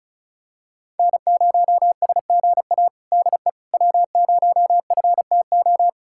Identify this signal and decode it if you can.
Morse code, heard as a keyed tone, about 34 wpm, 700 Hz. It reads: N0SGA DE W0FTO
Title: USA Callsign DE Callsign - 5 Characters - 35wpm